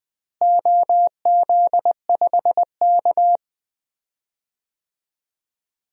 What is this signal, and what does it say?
Morse code, heard as a keyed tone, about 20 wpm, 700 Hz. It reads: OZ5K
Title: DXCC-100 Call Signs - 4 Characters - Repeated 3x - 20wpm